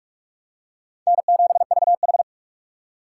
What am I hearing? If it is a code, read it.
Morse code, heard as a keyed tone, about 46 wpm, 700 Hz. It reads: N7VH
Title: USA Call Signs - 4 Characters - 6x Character Spacing - 45wpm